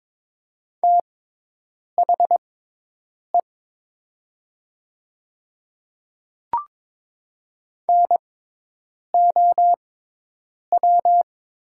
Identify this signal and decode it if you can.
Morse code, heard as a keyed tone, about 22 wpm, spaced as for 8 wpm, 700 Hz. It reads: THE NOW
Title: 6x Character Spacing - Top 100 Words - 22wpm